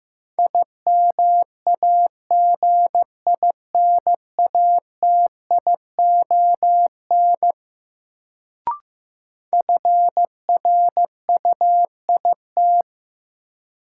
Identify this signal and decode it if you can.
Morse code, heard as a keyed tone, about 15 wpm, 700 Hz. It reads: IMAGINATION FRUIT
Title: Sentences from Top 1300 Words - New Word Drill - 15wpm